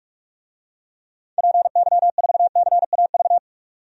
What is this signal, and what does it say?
Morse code, heard as a keyed tone, about 46 wpm, 700 Hz. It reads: PY4CAV
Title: DXCC-100 Call Signs - 6 Characters - 45wpm